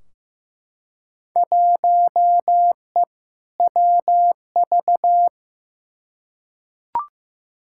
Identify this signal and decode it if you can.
Morse code, heard as a keyed tone, about 15 wpm, 700 Hz. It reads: 1E WV
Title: Field Day Exchanges Encore - 15wpm